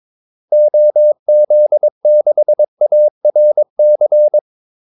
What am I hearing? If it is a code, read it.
Morse code, heard as a keyed tone, about 22 wpm, 600 Hz. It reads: OZ6ARC
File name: Callsigns-22wpm